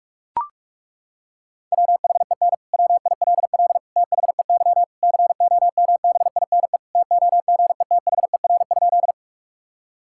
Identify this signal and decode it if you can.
Morse code, heard as a keyed tone, about 41 wpm, 700 Hz. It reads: WHEN WILL THEY COMBINE TOGETHER?
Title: Sentences from Top 1600 Words 45wpm